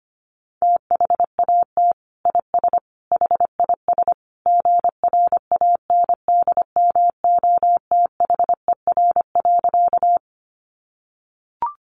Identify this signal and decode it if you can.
Morse code, heard as a keyed tone, about 25 wpm, 700 Hz. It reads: THAT IS HIS GRANDMOTHER.
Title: Sentences from Top 1200 Words 25wpm